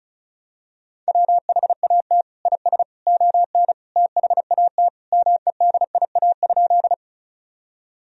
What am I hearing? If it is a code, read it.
Morse code, heard as a keyed tone, about 36 wpm, 700 Hz. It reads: WHAT IS ON THAT MEDIA?
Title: Sentences from Top 1000 Words - Review Part 4 - 35wpm